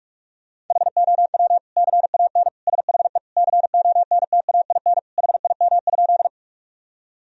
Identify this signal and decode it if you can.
Morse code, heard as a keyed tone, about 42 wpm, 700 Hz. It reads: HOW CAN SHE CONTAIN HIM?
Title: Sentences from Top 1000 Words - Review Part 2 - 45wpm